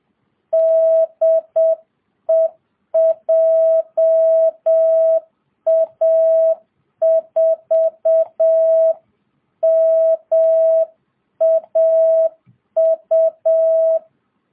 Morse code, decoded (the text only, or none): DE JA4MAU